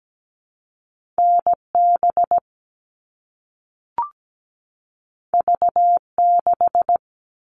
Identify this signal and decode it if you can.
Morse code, heard as a keyed tone, about 17 wpm, 700 Hz. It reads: NB V6
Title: All Cut Numbers - 2-Digit 17wpm